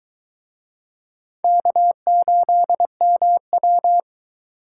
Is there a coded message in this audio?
Yes: Morse code from 1.4 to 4.0 s, heard as a keyed tone, about 21 wpm, 700 Hz. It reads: K8MW